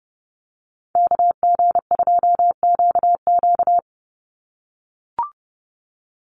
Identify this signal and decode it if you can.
Morse code, heard as a keyed tone, about 30 wpm, 700 Hz. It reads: KG2QQ
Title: USA Call Signs 30wpm